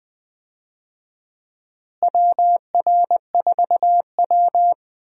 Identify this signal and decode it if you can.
Morse code, heard as a keyed tone, about 20 wpm, 700 Hz. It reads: WR4W
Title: USA Call Signs - 4 Characters - Repeated 3x - 20wpm